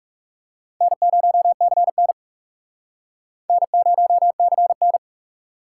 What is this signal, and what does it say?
Morse code, heard as a keyed tone, about 41 wpm, 700 Hz. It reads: N0CN N0CN